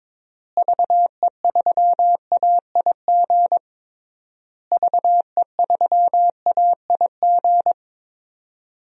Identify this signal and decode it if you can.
Morse code, heard as a keyed tone, about 22 wpm, 700 Hz. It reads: VE3AIG VE3AIG
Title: DXCC-100 Call Signs - 6 Characters - Repeated 3x - 22wpm